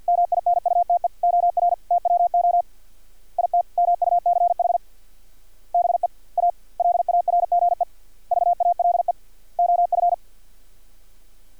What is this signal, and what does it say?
Morse code, heard as a keyed tone, about 50 wpm, 700 Hz. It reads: MINUTE OR TWO IT MUCH BE A LARGE VALE OF
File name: archived / CRSnet